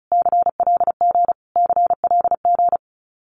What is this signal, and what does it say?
Morse code, heard as a keyed tone, about 34 wpm, 700 Hz. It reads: CLG CLG
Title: Abbreviations Repeated 3x - 35wpm